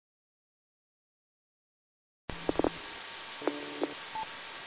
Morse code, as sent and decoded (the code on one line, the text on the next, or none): none